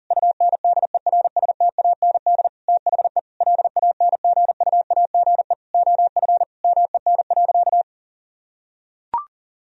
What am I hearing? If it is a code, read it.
Morse code, heard as a keyed tone, about 40 wpm, 700 Hz. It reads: UNDERSTAND THE LANGUAGE OF MEN.
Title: Sentences from Top 1000 Words - Review Part 3 - 40wpm